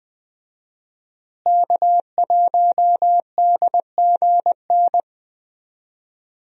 Morse code, heard as a keyed tone, about 20 wpm, 700 Hz. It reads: K1DGN
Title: USA Call Signs - 5 Characters - Repeated 2x - 20wpm